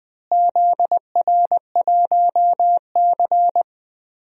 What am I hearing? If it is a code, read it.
Morse code, heard as a keyed tone, about 20 wpm, 700 Hz. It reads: ZR1C
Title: DXCC-100 Call Signs - 4 Characters - Repeated 3x - 20wpm